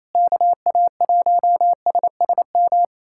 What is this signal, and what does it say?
Morse code, heard as a keyed tone, about 28 wpm, 700 Hz. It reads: KA1SSM
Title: USA Callsign DE Callsign - 6 Characters - 28wpm